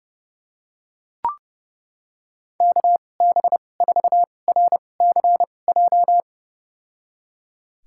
Morse code, heard as a keyed tone, about 31 wpm, spaced as for 22 wpm, 700 Hz. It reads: KB4RCJ
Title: USA Call Signs - 6 Characters - 2x Character Spacing - 30wpm